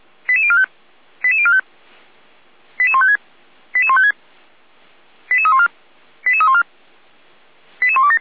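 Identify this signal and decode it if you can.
Five-tone selective call (ZVEI), heard as quick runs of five tones, tones of about 70 ms. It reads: 88045 88045 88156 88156 88314 88314 88127
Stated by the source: SDR-received ZVEI five-tone selective call.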